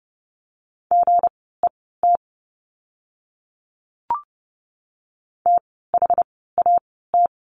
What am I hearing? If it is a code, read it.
Morse code, heard as a keyed tone, about 30 wpm, spaced as for 17 wpm, 700 Hz. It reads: GET THAT
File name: course / 30:10wpm / 230 Lesson 37f - Letter x - Words 30wpm